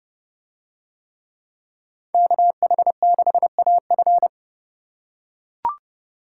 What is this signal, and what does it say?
Morse code, heard as a keyed tone, about 28 wpm, 700 Hz. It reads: KH6AF